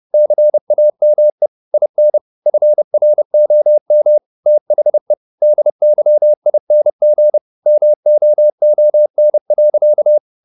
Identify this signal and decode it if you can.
Morse code, heard as a keyed tone, about 30 wpm, 600 Hz. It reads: CAME IN FROM THE DYING MOON.
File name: the old man and the sea / the old man and the sea 1 30wpm